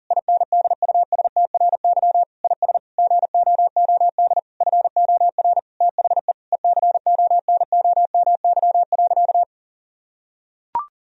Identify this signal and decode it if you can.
Morse code, heard as a keyed tone, about 37 wpm, 700 Hz. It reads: INDUSTRY IS GOOD FOR THE ECONOMY.
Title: Sentences from Top 1000 Words - Review Part 4 - 40wpm